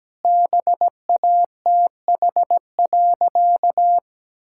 Morse code, heard as a keyed tone, about 17 wpm, 700 Hz. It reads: BATH.
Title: Sentences from Top 1600 Words 17wpm